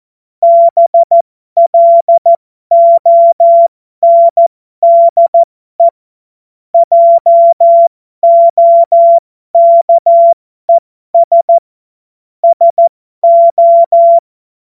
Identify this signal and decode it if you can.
Morse code, heard as a keyed tone, about 14 wpm, spaced as for 12 wpm, 680 Hz. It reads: BLONDE JOKES SO
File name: Morsekurs / Witze